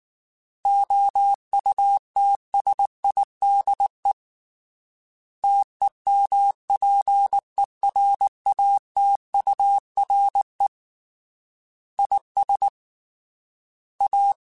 Morse code, heard as a keyed tone, about 19 wpm, 800 Hz. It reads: OUTSIDE TEMPERATURE IS A
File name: Weather Sets / Weather Sets-mp3s / Weather 4 10 @18